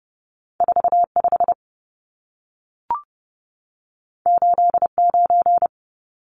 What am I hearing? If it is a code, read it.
Morse code, heard as a keyed tone, about 30 wpm, 700 Hz. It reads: 45 89
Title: Numbers - 2-Digit 30wpm